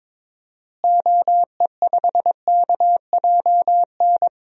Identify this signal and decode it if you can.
Morse code, heard as a keyed tone, about 21 wpm, 700 Hz. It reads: OE5KJN
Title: DXCC-100 Call Signs - 6 Characters - Repeated 3x - 22wpm